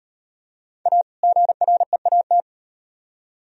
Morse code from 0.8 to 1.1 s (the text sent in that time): A